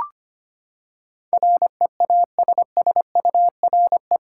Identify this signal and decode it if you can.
Morse code, heard as a keyed tone, about 25 wpm, 700 Hz. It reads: REASSURE
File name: Eight Letter Words - 25wpm